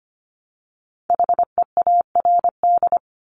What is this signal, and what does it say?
Morse code, heard as a keyed tone, about 25 wpm, 700 Hz. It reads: HEARD
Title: Multi-letter Phonemes Repeated 3x 25wpm